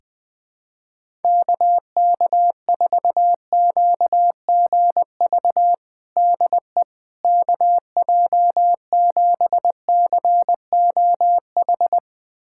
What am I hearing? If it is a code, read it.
Morse code, heard as a keyed tone, about 20 wpm, 700 Hz. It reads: KK4QGV DE KJ7COH
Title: USA Callsign DE Callsign 20wpm